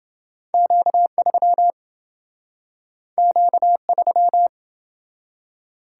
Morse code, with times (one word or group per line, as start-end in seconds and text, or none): 0.5-1.7 Q3
3.2-4.5 Q3